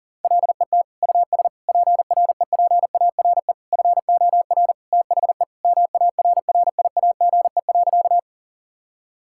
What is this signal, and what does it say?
Morse code, heard as a keyed tone, about 40 wpm, 700 Hz. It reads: LET US PREPARE FOR THE MARRIAGE.
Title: Sentences from Top 1000 Words - Review Part 6 - 40wpm